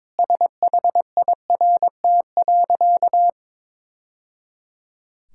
Morse code, heard as a keyed tone, about 21 wpm, 700 Hz. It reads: SHIRT.
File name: practice / 22wpm / Easy Sentences with Hints - 22wpm